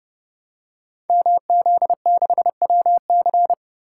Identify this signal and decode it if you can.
Morse code, heard as a keyed tone, about 28 wpm, 700 Hz. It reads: MZ6WC
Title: Code Groups - 5 Letter:Numbers - 30wpm